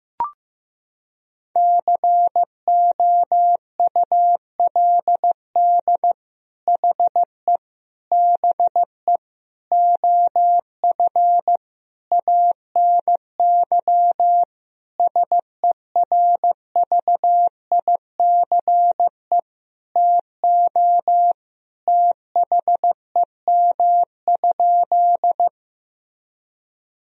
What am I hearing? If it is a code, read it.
Morse code, heard as a keyed tone, about 15 wpm, 700 Hz. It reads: COULD HE BE OF ANY SERVICE TO THEM?
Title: Sentences from Top 1000 Words - Review Part 4 - 15wpm